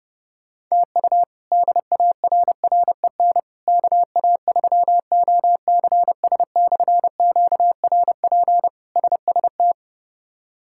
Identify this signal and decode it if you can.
Morse code, heard as a keyed tone, about 30 wpm, 700 Hz. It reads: TU DARREN KA3OCS/QRP SST
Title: K1USN SST QSO Practice - 30wpm